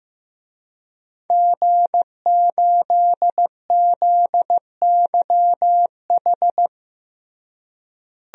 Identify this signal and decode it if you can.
Morse code, heard as a keyed tone, about 15 wpm, 700 Hz. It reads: G8ZYH
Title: DXCC-100 Call Signs - 5 Characters - Repeated 3x - 15wpm